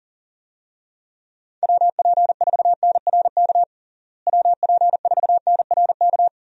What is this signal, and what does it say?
Morse code, heard as a keyed tone, about 40 wpm, 700 Hz. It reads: WP4NRK WP4NRK
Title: USA Call Signs - 6 Characters - Repeated 3x - 40wpm